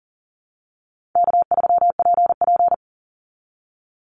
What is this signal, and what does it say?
Morse code, heard as a keyed tone, about 40 wpm, 700 Hz. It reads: K3PP
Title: Easy Call Signs Repeated 2x - 40wpm